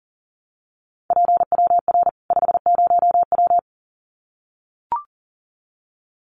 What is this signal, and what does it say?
Morse code, heard as a keyed tone, about 40 wpm, 700 Hz. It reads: PWR 50W
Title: Common QSO Phrases 40wpm